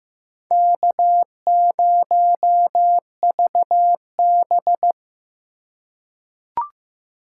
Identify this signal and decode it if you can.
Morse code, heard as a keyed tone, about 15 wpm, 700 Hz. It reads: K0VB
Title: USA Call Signs - 4 Characters - 4x Character Spacing - 15wpm